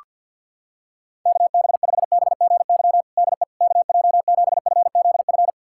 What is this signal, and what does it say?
Morse code, heard as a keyed tone, about 51 wpm, 700 Hz. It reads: KB5BGY DE KJ7FZF
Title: USA Callsign DE Callsign 50wpm